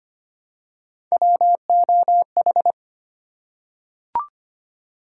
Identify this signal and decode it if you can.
Morse code, heard as a keyed tone, about 25 wpm, 700 Hz. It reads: WOH
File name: Three Letter-Number Combinations - 25wpm